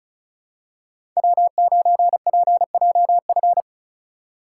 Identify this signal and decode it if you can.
Morse code, heard as a keyed tone, about 36 wpm, 700 Hz. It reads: W9PJF
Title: USA Call Signs - 5 Characters - 4x Character Spacing - 35wpm